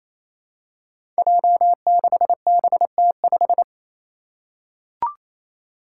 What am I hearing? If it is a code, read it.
Morse code, heard as a keyed tone, about 28 wpm, 700 Hz. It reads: J6BT5
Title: Five Challenging Letter-Number Combinations 28wpm